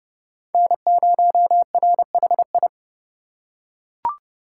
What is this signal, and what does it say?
Morse code, heard as a keyed tone, about 31 wpm, 700 Hz. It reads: N0RHI
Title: USA Call Signs - 3x Character Spacing - 30wpm